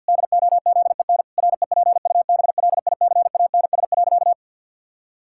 Morse code, heard as a keyed tone, about 50 wpm, 700 Hz. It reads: DOZEN REPUBLICANS.
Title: Pangram Challenge v2 - 50wpm